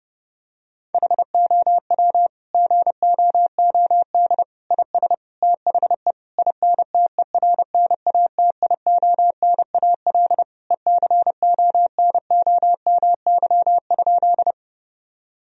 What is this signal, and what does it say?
Morse code, heard as a keyed tone, about 30 wpm, 700 Hz. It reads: HOW GOOD IS THE INTERNATIONAL ECONOMY?
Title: Sentences from Top 1000 Words - Review Part 4 - 30wpm